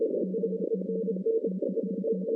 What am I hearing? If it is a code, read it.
Morse code, heard as a keyed tone, about 31 wpm, 190 Hz. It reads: CQ TEST
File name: 2016 arrl dx cw / 2016-ARRL-DX-CW-K5TR- - 0-30